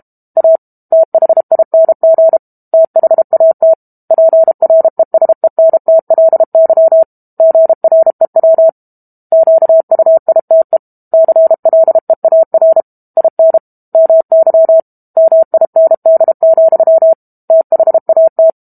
Morse code, heard as a keyed tone, about 32 wpm, 650 Hz. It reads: A THING THAT PRESENTLY GREW QUITE CLEAR IN MY MIND, THAT
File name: Book Two / 06-Chapter 6 - 32 WPM